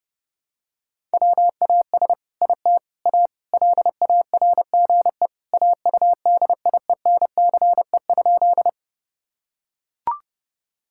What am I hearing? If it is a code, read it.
Morse code, heard as a keyed tone, about 30 wpm, 700 Hz. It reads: WAS IT A LARGE AUDIENCE?
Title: Sentences from Top 1000 Words - Review Part 5 - 30wpm